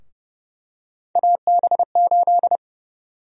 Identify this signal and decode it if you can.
Morse code, heard as a keyed tone, about 32 wpm, 700 Hz. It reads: AB8
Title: USA Call Sign Prefixes - Repeated 2x - 30wpm